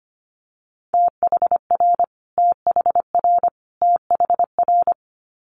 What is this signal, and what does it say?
Morse code, heard as a keyed tone, about 25 wpm, 700 Hz. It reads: THR THR THR
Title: Multi-letter Phonemes Repeated 3x 25wpm